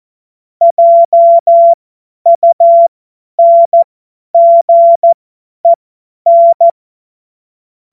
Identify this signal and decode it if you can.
Morse code, heard as a keyed tone, about 14 wpm, spaced as for 10 wpm, 680 Hz. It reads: JUNGEN
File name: Morsekurs / Witze / Witz 04. Oktober 2022 50 F70